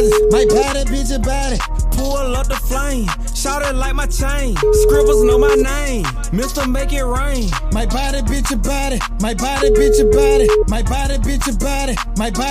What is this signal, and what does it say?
Ringback tone (Europe, 425 Hz), with noise.